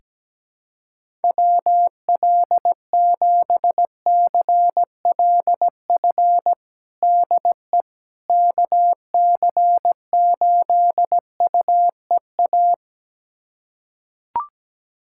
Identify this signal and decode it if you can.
Morse code, heard as a keyed tone, about 17 wpm, 700 Hz. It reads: WL7CLF DE KC8UEA
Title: USA Callsign DE Callsign - 6 Characters - 17wpm